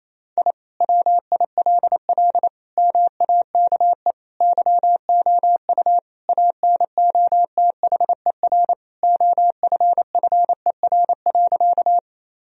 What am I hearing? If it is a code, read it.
Morse code, heard as a keyed tone, about 28 wpm, 700 Hz. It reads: I WILL MAKE YOU ANOTHER OFFER.
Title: Sentences from Top 1000 Words - Review Part 4 - 28wpm